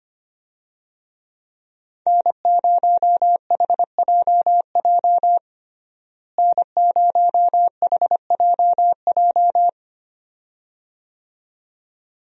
Morse code, heard as a keyed tone, about 25 wpm, 700 Hz. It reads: N0HJJ N0HJJ
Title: USA Call Signs - 5 Characters - Repeated 2x - 25wpm